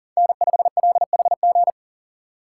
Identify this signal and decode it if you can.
Morse code, heard as a keyed tone, about 40 wpm, 700 Hz. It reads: N5LHG
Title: USA Call Signs - 5 Characters - 2x Character Spacing - 40wpm